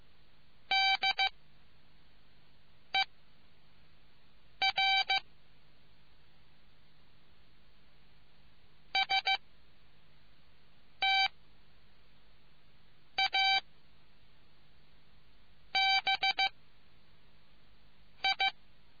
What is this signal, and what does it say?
Morse code, heard as a keyed tone, about 15 wpm, 770 Hz. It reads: D E R S T A B I